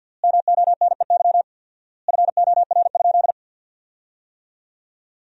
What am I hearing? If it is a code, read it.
Morse code, heard as a keyed tone, about 48 wpm, 700 Hz. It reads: MONEY FOR?
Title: 4x Word-Spacing - Sentences from Top 400 Words - 50wpm